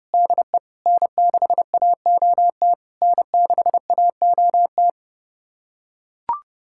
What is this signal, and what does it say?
Morse code, heard as a keyed tone, about 30 wpm, 700 Hz. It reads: DE N6AOT N6AOT
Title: SOTA QSO Practice 30wpm